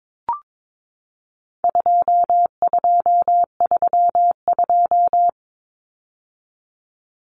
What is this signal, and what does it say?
Morse code, heard as a keyed tone, about 22 wpm, 700 Hz. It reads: 2232
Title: Common Cut Numbers - 4-Digit 22wpm